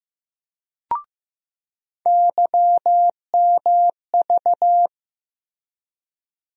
Morse code, heard as a keyed tone, about 15 wpm, 700 Hz. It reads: YMV
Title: USA Call Sign Postfixes - 3 Letters 15wpm